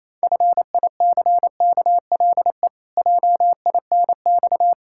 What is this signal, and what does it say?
Morse code, heard as a keyed tone, about 28 wpm, 700 Hz. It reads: FICKLE JINX
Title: Pangram Challenge v2 - 28wpm